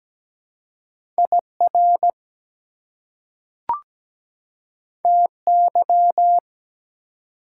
Morse code, heard as a keyed tone, about 17 wpm, 700 Hz. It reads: IR TY
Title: Prefixes and Suffixes 17wpm